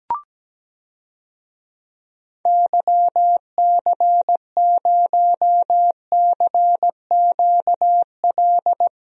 Morse code, heard as a keyed tone, about 17 wpm, 700 Hz. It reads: YC0CQL
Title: DXCC-100 Call Signs - 6 Characters - Repeated 3x - 17wpm